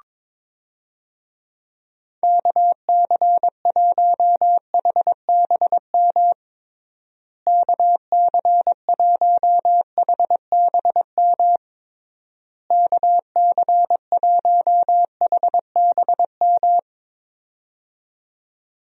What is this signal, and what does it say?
Morse code, heard as a keyed tone, about 22 wpm, 700 Hz. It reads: KC1HBM KC1HBM KC1HBM
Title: USA Call Signs - 6 Characters - Repeated 3x - 22wpm